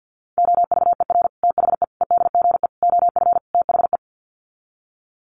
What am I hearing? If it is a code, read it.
Morse code, heard as a keyed tone, about 50 wpm, 700 Hz. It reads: OVER THE EDGE OF THE
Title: Sets of 5 Words Encore2 50wpm